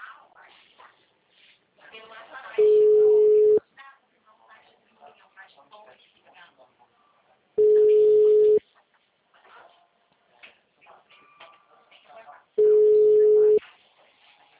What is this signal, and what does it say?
Ringback tone (Europe, 425 Hz), with speech.